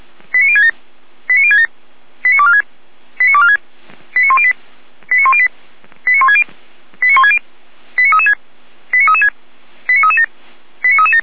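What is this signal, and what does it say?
Five-tone selective call (ZVEI), heard as quick runs of five tones, tones of about 70 ms. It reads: 89067 89067 89246 89246 89108 89108 89160 89160 89336 89336 89337 89337
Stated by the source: Five-tone ZVEI selective call 89560 received by SDR.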